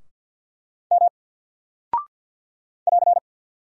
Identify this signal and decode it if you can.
Morse code, heard as a keyed tone, about 48 wpm, 700 Hz. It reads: M <AR>
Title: Single Character (Letter:Number:Punctuation:Prosign) 50wpm